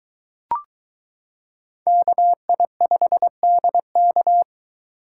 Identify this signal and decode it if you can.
Morse code, heard as a keyed tone, about 22 wpm, 700 Hz. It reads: KI5DK